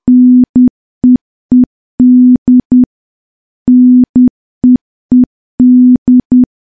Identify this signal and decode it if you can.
Morse code, heard as a keyed tone, about 10 wpm, 250 Hz. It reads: NEED NEED
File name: morse / morse-training-10WPM